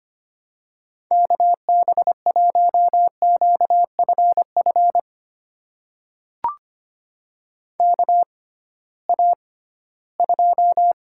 Morse code, heard as a keyed tone, about 25 wpm, spaced as for 16 wpm, 700 Hz. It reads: KB1QFF KA2
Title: USA Call Signs - 6 Characters - 6x Character Spacing - 25wpm